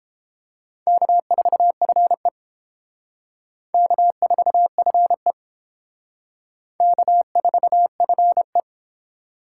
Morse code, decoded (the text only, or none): K4FE K4FE K4FE